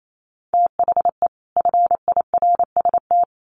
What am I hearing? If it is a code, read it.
Morse code, heard as a keyed tone, about 28 wpm, 700 Hz. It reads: THE FIRST
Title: Sets of 2 Words 28wpm